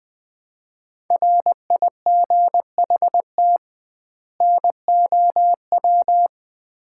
Morse code, heard as a keyed tone, about 20 wpm, 700 Hz. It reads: RIGHT NOW
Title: 2x Word-Spacing - Sets of 2 Words - 20wpm